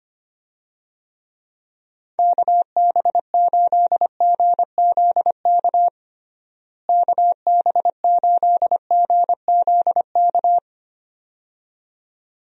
Morse code, heard as a keyed tone, about 25 wpm, 700 Hz. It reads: KB8GZK KB8GZK
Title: USA Call Signs - 6 Characters - Repeated 2x - 25wpm